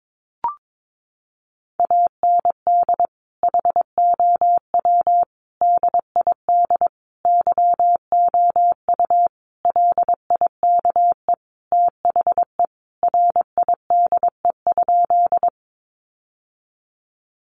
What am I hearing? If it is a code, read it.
Morse code, heard as a keyed tone, about 22 wpm, 700 Hz. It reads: AND HOW DID YOU LIKE THE RIDE?
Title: Sentences from Top 1200 Words 22wpm